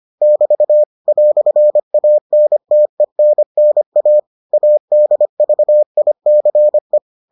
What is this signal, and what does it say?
Morse code, heard as a keyed tone, about 25 wpm, 600 Hz. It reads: <BT> "ANTENNA ADVICE